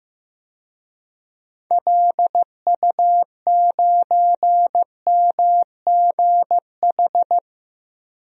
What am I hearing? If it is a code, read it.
Morse code, heard as a keyed tone, about 15 wpm, 700 Hz. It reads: LU9MGH
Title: DXCC-100 Call Signs - 6 Characters - 15wpm